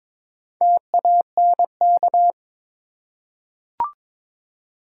Morse code, heard as a keyed tone, about 22 wpm, 700 Hz. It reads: TANK
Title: Four Letter Words - 22wpm